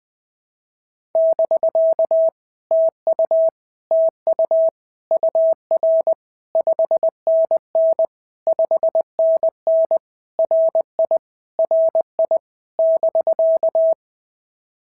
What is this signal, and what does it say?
Morse code, heard as a keyed tone, about 19 wpm, 650 Hz. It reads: <BK> TU TU UR 5NN 5NN RI RI <BK>
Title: POTA QSO Practice - 20wpm